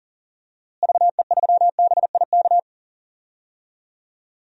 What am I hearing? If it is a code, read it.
Morse code, heard as a keyed tone, about 40 wpm, 700 Hz. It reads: VE3BIK
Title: DXCC-100 Call Signs - 6 Characters - Repeated 3x - 40wpm